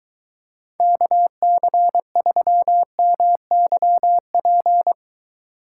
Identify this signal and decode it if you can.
Morse code, heard as a keyed tone, about 23 wpm, 700 Hz. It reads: KC3MYP